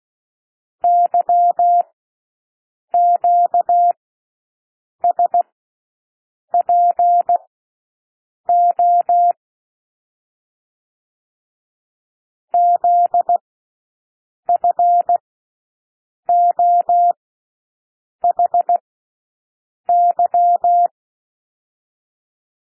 Morse code, heard as a keyed tone, about 16 wpm, spaced as for 6 wpm, 700 Hz. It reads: YQSPO ZFOHY
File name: cwmp3 / L1 SOUND LETTERS 16 wpm